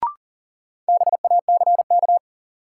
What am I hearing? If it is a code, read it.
Morse code, heard as a keyed tone, about 40 wpm, 700 Hz. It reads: BACK
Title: Sentences from Top 200 Words - New Word Drill - 40wpm